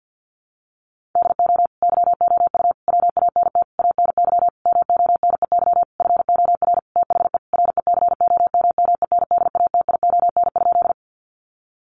Morse code, heard as a keyed tone, about 50 wpm, 700 Hz. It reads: DO YOU WANT ANY MONEY FOR THE RECOMMENDATION?
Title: Sentences from Top 1400 Words 50wpm